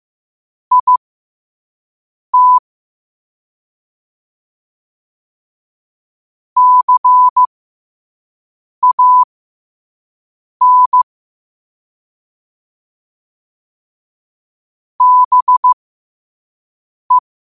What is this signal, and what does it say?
Morse code, heard as a keyed tone, about 15 wpm, spaced as for 5 wpm, 1000 Hz. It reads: IT CAN BE
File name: qotdfiles / 5wpm